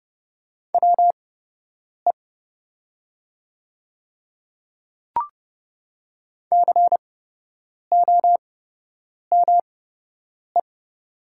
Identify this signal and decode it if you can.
Morse code, heard as a keyed tone, about 30 wpm, spaced as for 8 wpm, 700 Hz. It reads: WE COME
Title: 8x Character Spacing - Top 100 Words - 30wpm